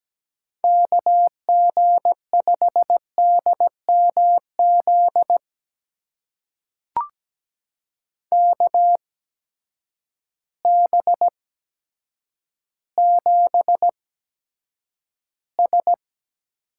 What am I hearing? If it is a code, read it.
Morse code, heard as a keyed tone, about 17 wpm, spaced as for 8 wpm, 700 Hz. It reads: KG5DMZ KB7S
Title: USA Call Signs - 6 Characters - 8x Character Spacing - 17wpm